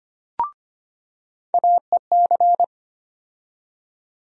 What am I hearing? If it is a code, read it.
Morse code, heard as a keyed tone, about 25 wpm, 700 Hz. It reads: AEC
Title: USA Call Sign Postfixes - 3 Letters 25wpm